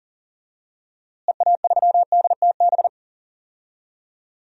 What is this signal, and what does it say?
Morse code, heard as a keyed tone, about 40 wpm, 700 Hz. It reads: EA3DTB
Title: DXCC-100 Call Signs - 6 Characters - Repeated 2x - 40wpm